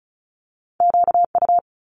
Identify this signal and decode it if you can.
Morse code, heard as a keyed tone, about 36 wpm, 700 Hz. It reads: QU
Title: Multi-letter Phonemes Quick Fire 35wpm